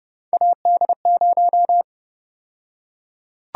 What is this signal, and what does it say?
Morse code, heard as a keyed tone, about 30 wpm, 700 Hz. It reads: AD0
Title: USA Call Sign Prefixes - Repeated 2x - 30wpm